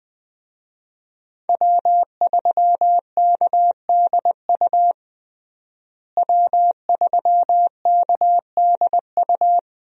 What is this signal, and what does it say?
Morse code, heard as a keyed tone, about 20 wpm, 700 Hz. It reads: W3KDU W3KDU
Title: USA Call Signs - 5 Characters - Repeated 3x - 20wpm